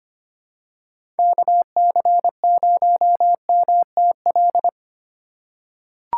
Morse code, heard as a keyed tone, about 25 wpm, 700 Hz. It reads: KC0MTL
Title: USA Call Signs - 6 Characters - 3x Character Spacing - 25wpm